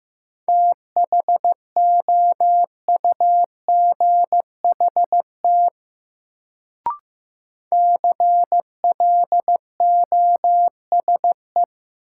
Morse code, heard as a keyed tone, about 15 wpm, 700 Hz. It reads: THOUGHT CLOSE
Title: Sentences from Top 300 Words - New Word Drill - 15wpm